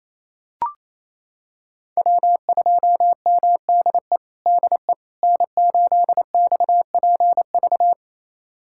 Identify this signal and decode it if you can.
Morse code, heard as a keyed tone, about 28 wpm, 700 Hz. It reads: W2MDE DE N8XPV
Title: USA Callsign DE Callsign - 5 Characters - 28wpm